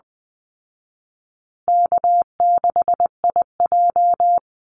Morse code, heard as a keyed tone, about 20 wpm, 700 Hz. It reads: K6IJ
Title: Easy Call Signs Repeated 3x - 20wpm